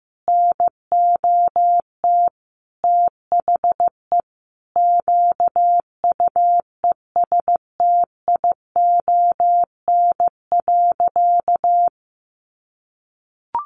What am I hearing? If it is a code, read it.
Morse code, heard as a keyed tone, about 15 wpm, 700 Hz. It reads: NOT THE QUESTION.
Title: Sentences from Top 600 Words 15wpm